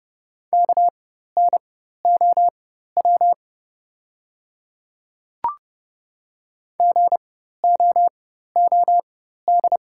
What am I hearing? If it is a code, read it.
Morse code, heard as a keyed tone, about 30 wpm, spaced as for 14 wpm, 700 Hz. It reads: KNOW GOOD
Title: 4x Character Spacing - Top 200 Words - 30wpm